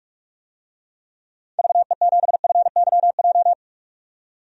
Morse code, heard as a keyed tone, about 45 wpm, 700 Hz. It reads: VE7FYJ
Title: DXCC-100 Call Signs - 6 Characters - 45wpm